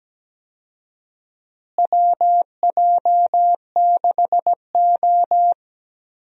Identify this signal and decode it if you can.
Morse code, heard as a keyed tone, about 17 wpm, 700 Hz. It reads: WJ6O
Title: USA Call Signs - 4 Characters - Repeated 2x - 17wpm